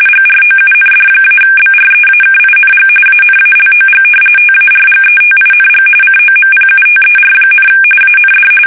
RTTY, about 45.45 baud. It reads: GROUND-BREAKING ANALOG TARGETING COMPUTER, AND
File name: fsk850 / battleshipnj